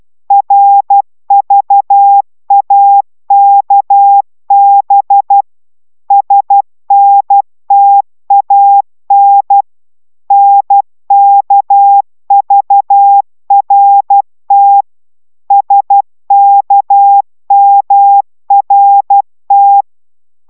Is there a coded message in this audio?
Yes: Morse code from 0.3 to 19.8 s, heard as a keyed tone, about 12 wpm, 800 Hz. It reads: RVAKB SNTAN NKVRT SKMRT